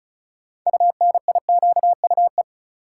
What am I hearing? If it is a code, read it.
Morse code, heard as a keyed tone, about 34 wpm, 700 Hz. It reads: UNIQUE
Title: Sentences from Top 1000 Words - Review Part 7 - New Word Drill - 35wpm